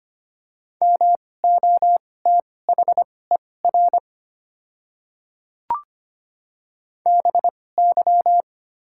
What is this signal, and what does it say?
Morse code, heard as a keyed tone, about 25 wpm, spaced as for 18 wpm, 700 Hz. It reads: MOTHER BY
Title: 2x Character Spacing - Top 200 Words - 25wpm